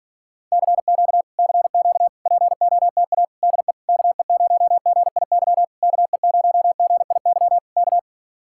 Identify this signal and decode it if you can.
Morse code, heard as a keyed tone, about 47 wpm, 700 Hz. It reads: CQ CQ POTA DE KE0GIY KE0GIY K